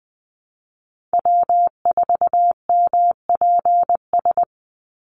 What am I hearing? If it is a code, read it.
Morse code, heard as a keyed tone, about 20 wpm, 700 Hz. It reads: W4MPS
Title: NA SOTA Chaser Callsigns 20wpm